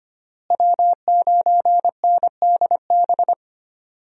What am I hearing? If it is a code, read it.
Morse code, heard as a keyed tone, about 25 wpm, 700 Hz. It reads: W9NDB
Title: USA Call Signs - 5 Characters - Repeated 3x - 25wpm